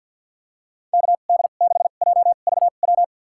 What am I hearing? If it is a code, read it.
Morse code, heard as a keyed tone, about 46 wpm, spaced as for 35 wpm, 700 Hz. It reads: KD6JVW
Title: USA Call Signs - 2x Character Spacing - 50wpm